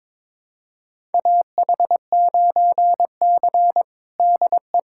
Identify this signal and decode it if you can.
Morse code, heard as a keyed tone, about 22 wpm, 700 Hz. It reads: AH9C DE
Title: USA Callsign DE Callsign - 4 Characters - 22wpm